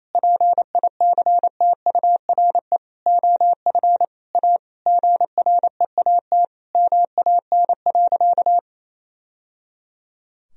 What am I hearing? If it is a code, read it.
Morse code, heard as a keyed tone, about 28 wpm, 700 Hz. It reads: PICTURE OF A GREAT MAN.
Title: Sentences from Top 1000 Words - Review Part 2 - 28wpm